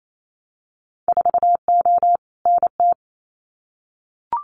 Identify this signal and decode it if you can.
Morse code, heard as a keyed tone, about 28 wpm, 700 Hz. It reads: 4O NT
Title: Winter Field Day Exchanges Encore 28wpm